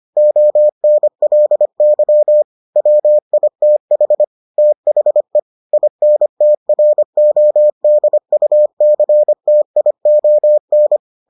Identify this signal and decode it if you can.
Morse code, heard as a keyed tone, about 25 wpm, 600 Hz. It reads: ONLY WITH THE INTRODUCTION